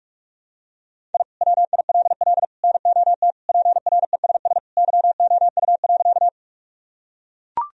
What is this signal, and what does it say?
Morse code, heard as a keyed tone, about 41 wpm, 700 Hz. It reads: I WILL NOT PRESS YOU.
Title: Sentences from Top 1000 Words - Review Part 1 - 45wpm